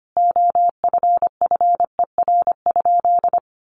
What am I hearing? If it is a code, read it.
Morse code, heard as a keyed tone, about 25 wpm, 700 Hz. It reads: OFFER?
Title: Sentences from Top 1300 Words 25wpm